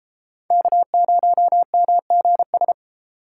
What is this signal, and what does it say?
Morse code, heard as a keyed tone, about 29 wpm, 700 Hz. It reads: K0MGS